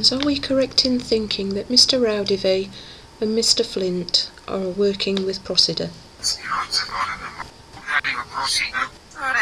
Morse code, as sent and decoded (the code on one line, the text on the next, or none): none